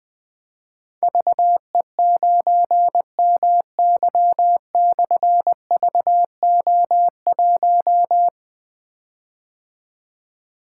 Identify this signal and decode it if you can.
Morse code, heard as a keyed tone, about 20 wpm, 700 Hz. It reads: VE9MY/VO1
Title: Difficult Call Signs 20wpm